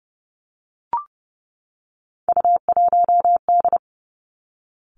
Morse code, heard as a keyed tone, about 31 wpm, 700 Hz. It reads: U1D
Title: Three Challenging Letter-Number Combinations 30wpm